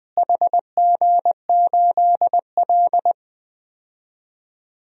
Morse code, heard as a keyed tone, about 21 wpm, 700 Hz. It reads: HG8L
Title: DXCC-100 Call Signs - 4 Characters - Repeated 3x - 20wpm